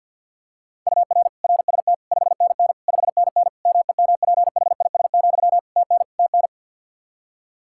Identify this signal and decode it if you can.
Morse code, heard as a keyed tone, about 51 wpm, 700 Hz. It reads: UR RST 5NN 5NN MEMPHIS, TN TN
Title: Practice QSO - Exchange 1 - Beginning - 50wpm